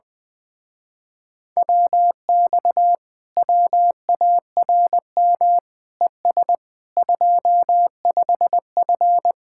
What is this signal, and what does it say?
Morse code, heard as a keyed tone, about 20 wpm, 700 Hz. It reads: WX WARM ES 25F
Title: Speed-Racing Common QSO Phrases - 17wpm